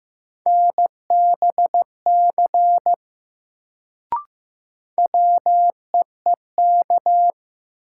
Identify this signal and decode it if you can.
Morse code, heard as a keyed tone, about 15 wpm, 700 Hz. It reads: NBC WEEK
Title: Sets of 5 Words Encore - New Word Drill - 15wpm